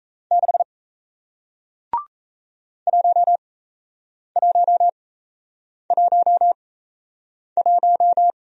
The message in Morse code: 6 1111

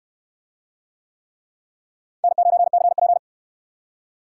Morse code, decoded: N9CP

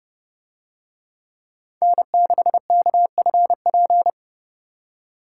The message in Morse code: N6KFP

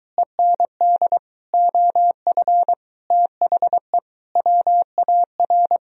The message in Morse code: END OF THE WAR

